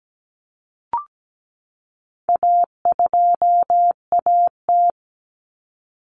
Morse code, heard as a keyed tone, about 17 wpm, 700 Hz. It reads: A2AT
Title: Common Cut Numbers - 4-Digit 17wpm